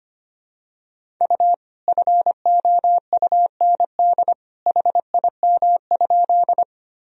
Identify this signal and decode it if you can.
Morse code, heard as a keyed tone, about 25 wpm, 700 Hz. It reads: U FOUND HIM?
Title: Sentences from Top 300 Words and Abbreviations 25wpm